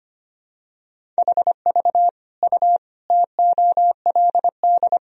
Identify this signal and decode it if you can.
Morse code, heard as a keyed tone, about 25 wpm, 700 Hz. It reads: HV U TOLD
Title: Sentences from Top 400 Words and Abbreviations 25wpm